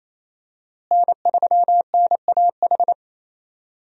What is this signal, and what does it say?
Morse code, heard as a keyed tone, about 28 wpm, 700 Hz. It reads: N3NAH
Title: USA Call Signs - 5 Characters - Repeated 2x - 28wpm